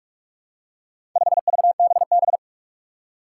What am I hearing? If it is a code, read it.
Morse code, heard as a keyed tone, about 46 wpm, 700 Hz. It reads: 5VBB
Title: Four Challenging Letter-Number Combinations 45wpm